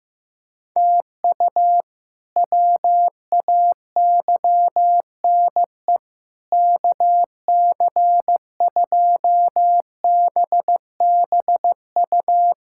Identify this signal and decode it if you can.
Morse code, heard as a keyed tone, about 15 wpm, 700 Hz. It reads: TU WAYNE KC2BBU
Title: K1USN SST QSO Practice - 15wpm